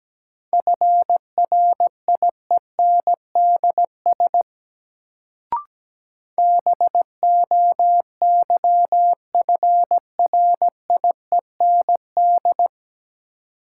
Morse code, heard as a keyed tone, about 17 wpm, 700 Hz. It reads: FRIENDS BOYFRIEND
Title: Sentences from Top 1000 Words - Review Part 5 - New Word Drill - 17wpm